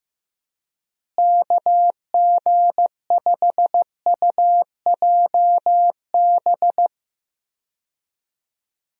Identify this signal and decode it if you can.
Morse code, heard as a keyed tone, about 15 wpm, 700 Hz. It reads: KG5UJB
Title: USA Call Signs - 6 Characters - Repeated 3x - 15wpm